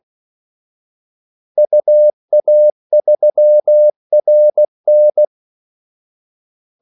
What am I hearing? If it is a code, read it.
Morse code, heard as a keyed tone, about 16 wpm, 600 Hz. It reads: UA3RN
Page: Kaldesignaler 16 wpm